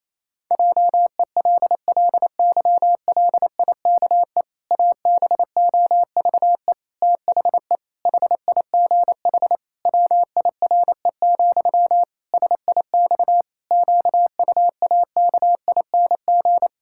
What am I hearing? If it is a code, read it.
Morse code, heard as a keyed tone, about 27 wpm, 700 Hz. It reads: JELLYLIKE ABOVE THE HIGH WIRE, SIX QUAKING
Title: Pangram Challenge v2 - 28wpm